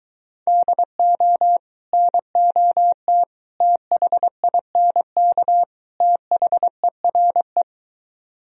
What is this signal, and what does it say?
Morse code, heard as a keyed tone, about 23 wpm, 700 Hz. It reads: DO NOT THINK THERE